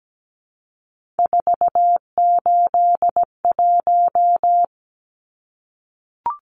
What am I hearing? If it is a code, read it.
Morse code, heard as a keyed tone, about 17 wpm, 700 Hz. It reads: 481
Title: Numbers - 3-Digit 17wpm